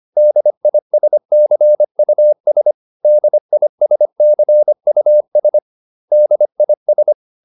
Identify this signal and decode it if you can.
Morse code, heard as a keyed tone, about 25 wpm, 600 Hz. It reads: DISCUS DISCUS DIS